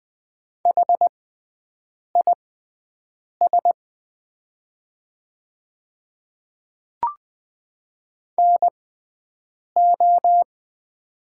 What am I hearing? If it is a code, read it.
Morse code, heard as a keyed tone, about 20 wpm, spaced as for 7 wpm, 700 Hz. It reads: HIS NO
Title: 6x Character Spacing - Top 100 Words - 20wpm